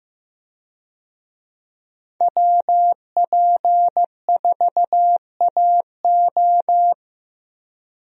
Morse code, heard as a keyed tone, about 15 wpm, 700 Hz. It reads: WP4AO
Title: USA Call Signs - 5 Characters - 15wpm